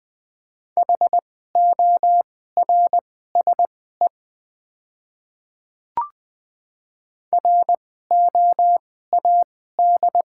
Morse code, heard as a keyed tone, about 20 wpm, spaced as for 14 wpm, 700 Hz. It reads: HORSE ROAD
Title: 2x Character Spacing - Top 500 Words - 20wpm